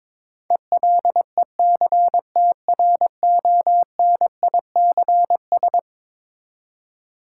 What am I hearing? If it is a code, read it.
Morse code, heard as a keyed tone, about 21 wpm, 700 Hz. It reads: ELECTRONICS
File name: practice / 22wpm / Eleven Letter Words - 22wpm